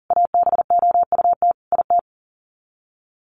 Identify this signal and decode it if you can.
Morse code, heard as a keyed tone, about 40 wpm, 700 Hz. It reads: ABOUT IT
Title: Sets of 5 Words 40wpm